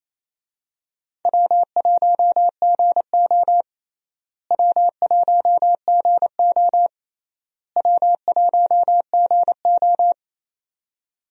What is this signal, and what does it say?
Morse code, heard as a keyed tone, about 28 wpm, 700 Hz. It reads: W1GO W1GO W1GO
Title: USA Call Signs - 4 Characters - Repeated 3x - 28wpm